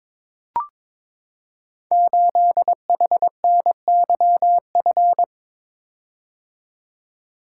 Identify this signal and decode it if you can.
Morse code, heard as a keyed tone, about 22 wpm, 700 Hz. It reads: 8HNYF
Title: Code Groups - 5 Letter:Numbers - 22wpm